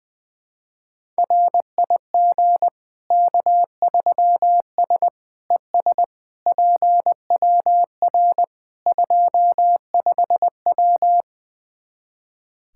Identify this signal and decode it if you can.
Morse code, heard as a keyed tone, about 20 wpm, 700 Hz. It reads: RIG K3S ES PWR 25W
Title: Practice QSO - Exchange 2 - Info - 20wpm